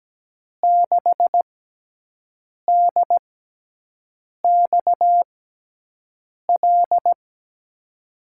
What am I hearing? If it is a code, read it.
Morse code, heard as a keyed tone, about 17 wpm, spaced as for 6 wpm, 700 Hz. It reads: 6DXL
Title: USA Call Signs - 6x Character Spacing - 17wpm